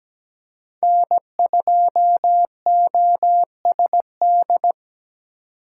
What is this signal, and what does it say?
Morse code, heard as a keyed tone, about 17 wpm, 700 Hz. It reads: N2OSD
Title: USA Call Signs - 2x Character Spacing - 17wpm